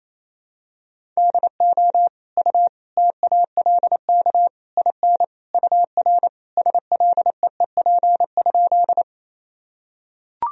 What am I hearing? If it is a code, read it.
Morse code, heard as a keyed tone, about 28 wpm, 700 Hz. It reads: DO U TALK IN UR SLEEP?
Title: Sentences from Top 500 Words and Abbreviations 28wpm